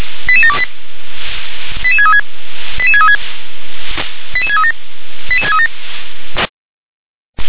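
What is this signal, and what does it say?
Five-tone selective call (ZVEI), heard as quick runs of five tones, tones of about 70 ms. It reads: 88728 88636 88636 88637 88637
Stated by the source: ZVEI five-tone call received by SDR.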